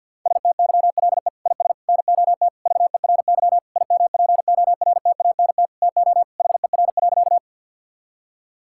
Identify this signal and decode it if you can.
Morse code, heard as a keyed tone, about 50 wpm, 700 Hz. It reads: STYLE IS NOT VERY IMPORTANT TO HER.
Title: Sentences from Top 1000 Words - Review Part 5 - 50wpm